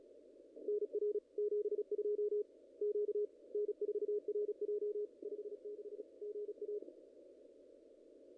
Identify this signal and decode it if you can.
Morse code, heard as a keyed tone, about 34 wpm, 410 Hz. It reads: NR 72 Q N4RJ 56 GA
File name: contests / ss-cw-k5tr-op-n5rz -0100